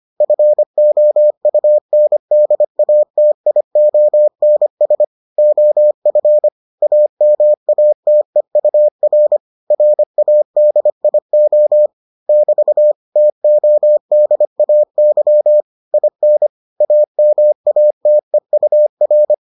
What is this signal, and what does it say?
Morse code, heard as a keyed tone, about 25 wpm, 600 Hz. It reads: FOUNDATIONS OF AMATEUR RADIO <BT> TODAY IN AMATEUR